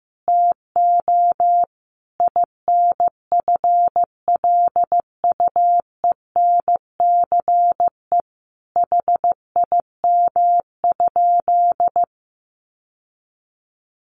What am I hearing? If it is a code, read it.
Morse code, heard as a keyed tone, about 15 wpm, 700 Hz. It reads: TO INFLUENCE HIM?
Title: Sentences from Top 1100 Words 15wpm